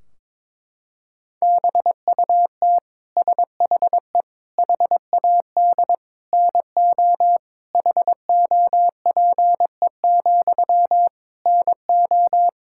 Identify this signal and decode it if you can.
Morse code, heard as a keyed tone, about 22 wpm, 700 Hz. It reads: BUT SHE HAD NO HOPE, NO
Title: Sentences from Top 1600 Words 22wpm